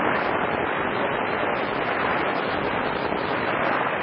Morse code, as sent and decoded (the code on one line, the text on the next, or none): none